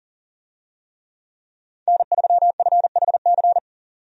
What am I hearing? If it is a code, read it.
Morse code, heard as a keyed tone, about 40 wpm, 700 Hz. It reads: N3FHC
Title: USA Call Signs - 5 Characters - 40wpm